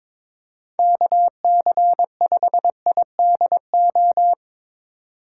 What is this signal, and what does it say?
Morse code, heard as a keyed tone, about 22 wpm, 700 Hz. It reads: KC5IDO
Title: USA Call Signs - 6 Characters - 2x Character Spacing - 22wpm